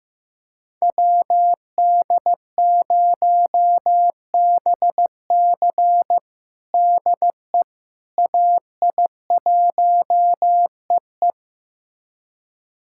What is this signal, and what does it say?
Morse code, heard as a keyed tone, about 15 wpm, 700 Hz. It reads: WD0BC DE AI1EE
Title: USA Callsign DE Callsign - 5 Characters - 15wpm